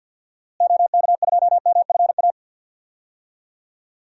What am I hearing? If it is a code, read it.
Morse code, heard as a keyed tone, about 49 wpm, 700 Hz. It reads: OK2MFA
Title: DXCC-100 Call Signs - 6 Characters - Repeated 3x - 50wpm